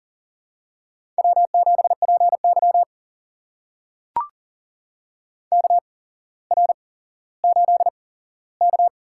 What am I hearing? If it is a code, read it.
Morse code, heard as a keyed tone, about 43 wpm, spaced as for 17 wpm, 700 Hz. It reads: W7PY KR8K
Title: USA Call Signs - 4 Characters - 8x Character Spacing - 40wpm